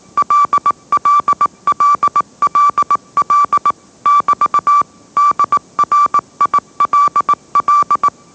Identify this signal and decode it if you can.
Morse code, heard as a keyed tone, about 18 wpm, 1160 Hz. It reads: LLLLL <BT> DRILL
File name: mrx / mrx082 20wpm